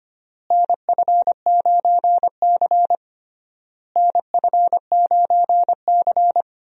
Morse code, heard as a keyed tone, about 25 wpm, 700 Hz. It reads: NF9C NF9C
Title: USA Call Signs - 4 Characters - Repeated 3x - 25wpm